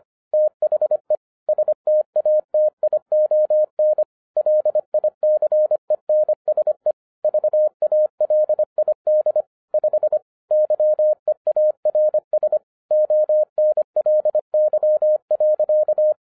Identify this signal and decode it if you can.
Morse code, heard as a keyed tone, about 26 wpm, 610 Hz. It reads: THE STATION LICENSE VALID 5 YEARS ONLY.